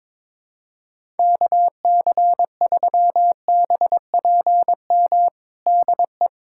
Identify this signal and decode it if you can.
Morse code, heard as a keyed tone, about 22 wpm, 700 Hz. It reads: KC3BPM DE
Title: USA Callsign DE Callsign - 6 Characters - 22wpm